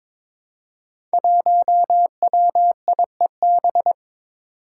Morse code, heard as a keyed tone, about 22 wpm, 700 Hz. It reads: 1WIEB
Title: Five Challenging Letter-Number Combinations 22wpm